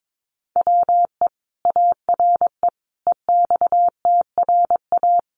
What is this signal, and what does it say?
Morse code, heard as a keyed tone, about 22 wpm, 700 Hz. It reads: WE ARE EXTRA
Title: Sentences from Top 1000 Words 22wpm